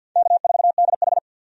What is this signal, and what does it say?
Morse code, heard as a keyed tone, about 47 wpm, 700 Hz. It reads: K4DH
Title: USA Call Signs - 4 Characters - 50wpm